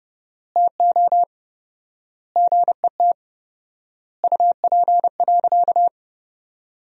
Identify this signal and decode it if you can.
Morse code, heard as a keyed tone, about 32 wpm, 700 Hz. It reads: TO GET UP.
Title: 4x Word-Spacing - Sentences from Top 500 Words - 30wpm